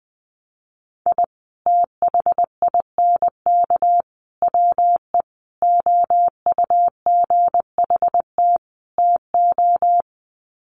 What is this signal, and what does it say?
Morse code, heard as a keyed tone, about 20 wpm, 700 Hz. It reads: I THINK WE OUGHT TO
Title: Sets of 5 Words Encore 20wpm